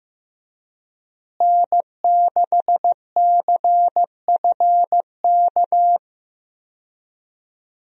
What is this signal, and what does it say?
Morse code, heard as a keyed tone, about 15 wpm, 700 Hz. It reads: N6CFK
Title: USA Call Signs - 5 Characters - Repeated 3x - 15wpm